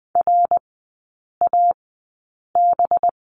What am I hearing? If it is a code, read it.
Morse code, heard as a keyed tone, about 20 wpm, spaced as for 8 wpm, 700 Hz. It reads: RAB
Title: 205 Lesson 33e - Letter j - Three Characters 20wpm